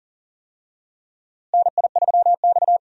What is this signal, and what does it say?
Morse code, heard as a keyed tone, about 40 wpm, 700 Hz. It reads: NI3X
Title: USA Call Signs - 4 Characters - Repeated 2x - 40wpm